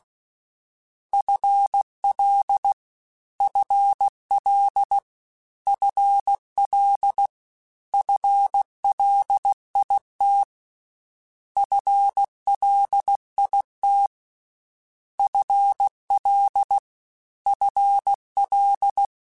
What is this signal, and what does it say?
Morse code, heard as a keyed tone, about 16 wpm, 800 Hz. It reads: FL FL FL FLIT FLIT FL FL
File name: FL / FL mp3s / FL Prefixes 3x2 @15